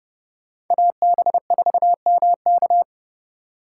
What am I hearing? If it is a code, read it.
Morse code, heard as a keyed tone, about 29 wpm, 700 Hz. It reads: AB4MK